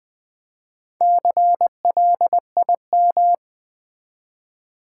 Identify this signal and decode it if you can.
Morse code, heard as a keyed tone, about 20 wpm, 700 Hz. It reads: CLIM
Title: Build A Word - 1 Letter At A Time - 5-Letter Words - 20wpm